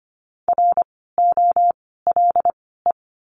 Morse code, heard as a keyed tone, about 25 wpm, spaced as for 16 wpm, 700 Hz. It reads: ROLE
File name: course / 25:10wpm / 094 Lesson 16f - Letter c - Words 25wpm